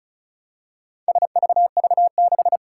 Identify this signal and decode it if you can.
Morse code, heard as a keyed tone, about 36 wpm, 700 Hz. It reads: SVV6
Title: Four Challenging Letter-Number Combinations 35wpm